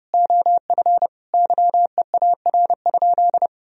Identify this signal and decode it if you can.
Morse code, heard as a keyed tone, about 31 wpm, 700 Hz. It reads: OF YEAR?
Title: Sentences from Top 1000 Words - Review Part 3 - 30wpm